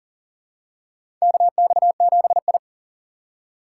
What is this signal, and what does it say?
Morse code, heard as a keyed tone, about 40 wpm, 700 Hz. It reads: KX7I
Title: USA Call Signs - 4 Characters - 2x Character Spacing - 40wpm